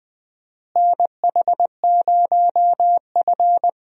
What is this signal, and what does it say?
Morse code, heard as a keyed tone, about 21 wpm, 700 Hz. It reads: NH0F
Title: USA Call Signs - 4 Characters - Repeated 2x - 20wpm